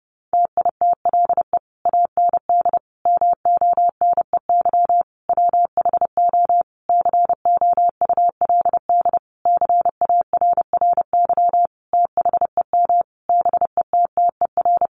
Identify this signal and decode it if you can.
Morse code, heard as a keyed tone, about 30 wpm, 700 Hz. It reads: TITLE AND MONEY WHO COULD CARRY THEM BETTER
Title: Sentences from Top 1200 Words 30wpm